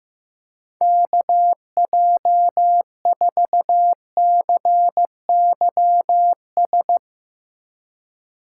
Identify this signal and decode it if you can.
Morse code, heard as a keyed tone, about 15 wpm, 700 Hz. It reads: KJ4CYS
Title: USA Call Signs - 6 Characters - 4x Character Spacing - 15wpm